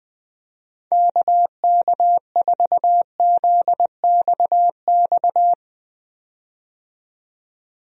Morse code, heard as a keyed tone, about 20 wpm, 700 Hz. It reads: KK4ZXX
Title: USA Call Signs - 6 Characters - 20wpm